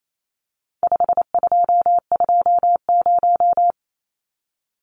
Morse code, heard as a keyed tone, about 28 wpm, 700 Hz. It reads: 5220
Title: Numbers - 4-Digit 28wpm